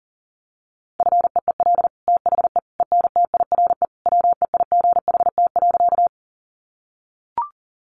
Morse code, heard as a keyed tone, about 40 wpm, 690 Hz. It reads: FEEL THE ENTIRE WEIGHT.
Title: Sentences from Top 800 Words 40wpm